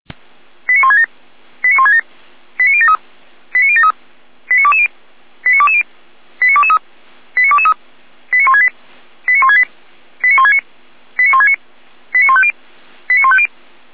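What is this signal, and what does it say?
Five-tone selective call (ZVEI), heard as quick runs of five tones, tones of about 70 ms. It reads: 89167 89167 89073 89073 89220 89220 89223 89223 89168 89168 89169 89169 89150 89150
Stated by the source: ZVEI five-tone selective call received on SDR.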